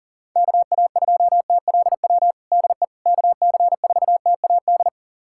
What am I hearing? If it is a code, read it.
Morse code, heard as a keyed tone, about 40 wpm, 690 Hz. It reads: KA2TLW DE KC4TAD
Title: USA Callsign DE Callsign - 6 Characters - 40wpm